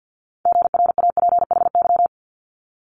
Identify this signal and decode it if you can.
Morse code, heard as a keyed tone, about 50 wpm, 700 Hz. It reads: GRAPHY
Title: Multi-letter Phonemes Quick Fire 50wpm